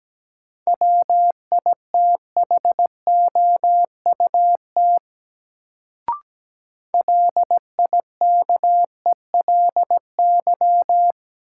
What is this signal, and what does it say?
Morse code, heard as a keyed tone, about 17 wpm, 700 Hz. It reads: WITHOUT LIKELY
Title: Sentences from Top 600 Words - New Word Drill - 17wpm